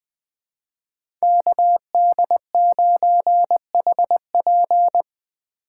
Morse code, heard as a keyed tone, about 20 wpm, 700 Hz. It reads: KD9HP
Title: USA Call Signs - 2x Character Spacing - 20wpm